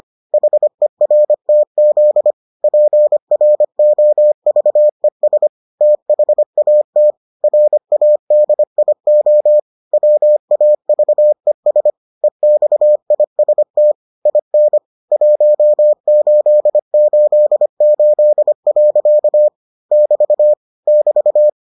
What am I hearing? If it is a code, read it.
Morse code, heard as a keyed tone, about 25 wpm, 600 Hz. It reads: HERTZ PROVES THAT RADIO WAVES EXIST IN 1888. <BT> <BT>